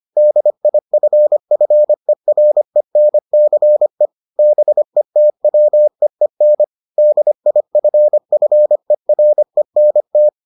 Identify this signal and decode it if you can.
Morse code, heard as a keyed tone, about 25 wpm, 600 Hz. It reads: DIFFERENCE BETWEEN DIFFERENT